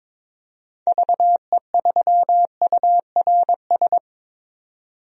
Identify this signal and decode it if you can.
Morse code, heard as a keyed tone, about 23 wpm, 700 Hz. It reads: VE3URS
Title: DXCC-100 Call Signs - 6 Characters - 22wpm